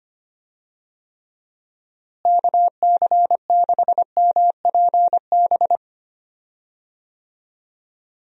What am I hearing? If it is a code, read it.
Morse code, heard as a keyed tone, about 25 wpm, 700 Hz. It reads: KC6MPB
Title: USA Call Signs - 6 Characters - Repeated 3x - 25wpm